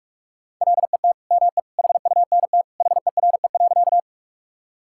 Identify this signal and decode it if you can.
Morse code, heard as a keyed tone, about 46 wpm, 700 Hz. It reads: LET ME HUNT HERE.
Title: Sentences from Top 1600 Words 45wpm